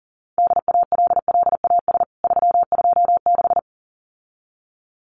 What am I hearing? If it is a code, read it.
Morse code, heard as a keyed tone, about 40 wpm, 700 Hz. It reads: DALLAS 326
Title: CWT Exchange 40wpm